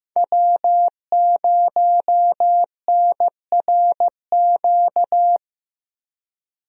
Morse code, heard as a keyed tone, about 15 wpm, 700 Hz. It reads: W0NRQ
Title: USA Call Signs - 5 Characters - 15wpm